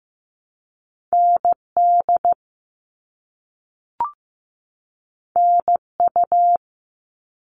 Morse code, heard as a keyed tone, about 15 wpm, 700 Hz. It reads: ND NU
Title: All Cut Numbers - 2-Digit 15wpm